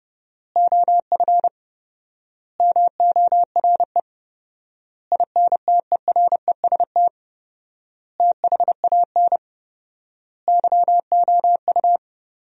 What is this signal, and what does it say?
Morse code, heard as a keyed tone, about 30 wpm, 700 Hz. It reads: OF MORE INTEREST THAN YOU
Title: 4x Word-Spacing - Sentences from Top 400 Words - 30wpm